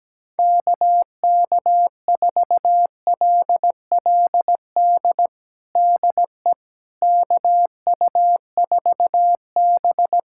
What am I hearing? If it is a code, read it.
Morse code, heard as a keyed tone, about 17 wpm, 700 Hz. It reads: KK4LLD DE KU4B
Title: USA Callsign DE Callsign 17wpm